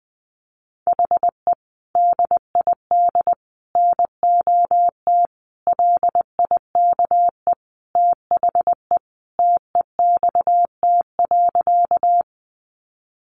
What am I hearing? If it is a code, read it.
Morse code, heard as a keyed tone, about 20 wpm, 700 Hz. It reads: HE DID NOT LIKE THE TEXT.
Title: Sentences from Top 1200 Words 20wpm